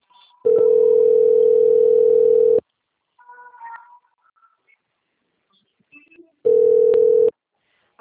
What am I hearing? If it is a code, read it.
Ringback tone (North America), 440 + 480 Hz.